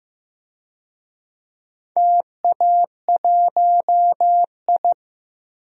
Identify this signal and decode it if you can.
Morse code, heard as a keyed tone, about 15 wpm, 700 Hz. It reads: TA1I
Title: DXCC-100 Call Signs - 4 Characters - Repeated 3x - 15wpm